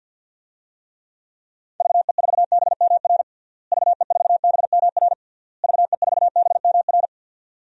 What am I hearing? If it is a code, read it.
Morse code, heard as a keyed tone, about 51 wpm, 700 Hz. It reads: VE4BMR VE4BMR VE4BMR
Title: DXCC-100 Call Signs - 6 Characters - Repeated 3x - 50wpm